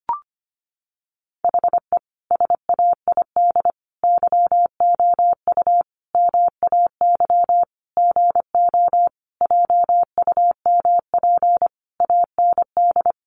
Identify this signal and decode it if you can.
Morse code, heard as a keyed tone, about 25 wpm, 700 Hz. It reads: HE SAID YOU MAY GO JUMP AND
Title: Sentences from Top 1300 Words 25wpm